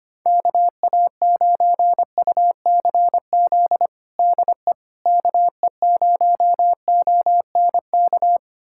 Morse code, heard as a keyed tone, about 25 wpm, 700 Hz. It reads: KA9UCZ DE KE0ONK
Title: USA Callsign DE Callsign - 6 Characters - 25wpm